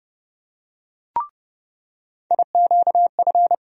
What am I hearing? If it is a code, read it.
Morse code, heard as a keyed tone, about 30 wpm, 700 Hz. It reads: IQF
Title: USA Call Sign Postfixes - 3 Letters 30wpm